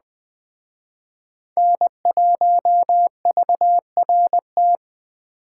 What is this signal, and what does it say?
Morse code, heard as a keyed tone, about 20 wpm, 700 Hz. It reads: N1VRT
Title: Speed Racing - USA 1x3 Call Signs 17wpm